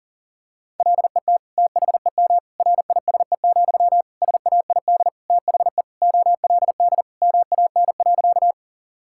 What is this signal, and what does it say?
Morse code, heard as a keyed tone, about 40 wpm, 700 Hz. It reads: LET THEM RISE, SAID THE OLD MAN.
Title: Sentences from Top 1000 Words - Review Part 6 - 40wpm